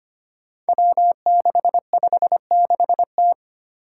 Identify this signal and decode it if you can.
Morse code, heard as a keyed tone, about 25 wpm, 700 Hz. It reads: W656T
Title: Five Challenging Letter-Number Combinations 25wpm